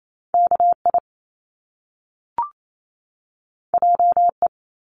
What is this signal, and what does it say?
Morse code, heard as a keyed tone, about 28 wpm, 700 Hz. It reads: KI JE
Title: Top Two Letter Combinations in Words 28wpm